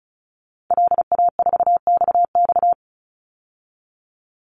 Morse code, heard as a keyed tone, about 34 wpm, 700 Hz. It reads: LA4XX
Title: Call Signs - 35wpm